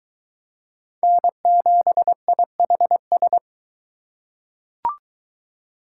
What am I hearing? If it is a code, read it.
Morse code, heard as a keyed tone, about 22 wpm, 700 Hz. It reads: N7IHS